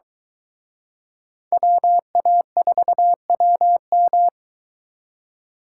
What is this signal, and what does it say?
Morse code, heard as a keyed tone, about 23 wpm, 700 Hz. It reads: WA4WM